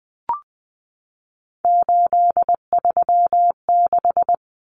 Morse code, heard as a keyed tone, about 20 wpm, 700 Hz. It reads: 836
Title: Numbers - 3-Digit 20wpm